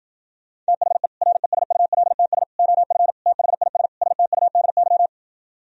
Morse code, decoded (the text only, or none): THE RESULTS OF THIS STUDY